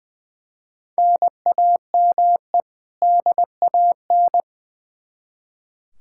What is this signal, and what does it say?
Morse code, heard as a keyed tone, about 20 wpm, 700 Hz. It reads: NAME DAN
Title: Practice QSO - Exchange 1 - Beginning - 20wpm